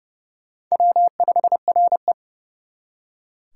Morse code, heard as a keyed tone, about 40 wpm, 700 Hz. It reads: W5RE